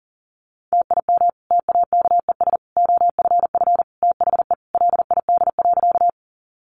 Morse code, heard as a keyed tone, about 40 wpm, 700 Hz. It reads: TIM TAKES OFF THE LID.
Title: Easy Phrases 40wpm